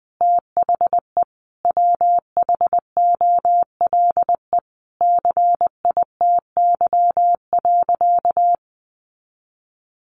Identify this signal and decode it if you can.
Morse code, heard as a keyed tone, about 20 wpm, 700 Hz. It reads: THE WHOLE CITY.
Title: Sentences from Top 1400 Words 20wpm